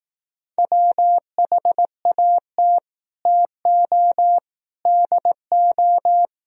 Morse code, heard as a keyed tone, about 17 wpm, 700 Hz. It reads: WHAT TO DO
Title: Speed-Racing - Sets of 3 Words 15wpm